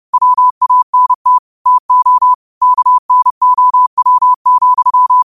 Morse code, heard as a keyed tone, about 30 wpm, 1000 Hz. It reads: WANT TO KNOW,